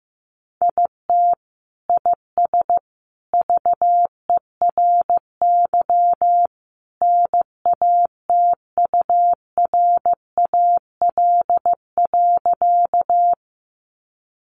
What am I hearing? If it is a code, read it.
Morse code, heard as a keyed tone, about 15 wpm, 700 Hz. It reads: IT IS VERY NATURAL.
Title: Sentences from Top 700 Words 15wpm